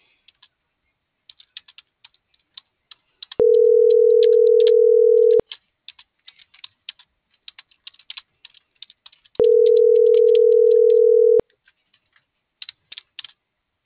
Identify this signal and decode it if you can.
Ringback tone (North America), 440 + 480 Hz.